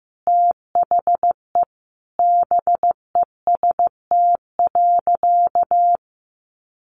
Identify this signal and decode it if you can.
Morse code, heard as a keyed tone, about 15 wpm, 700 Hz. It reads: THE BEST.
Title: Sentences from Top 1000 Words 15wpm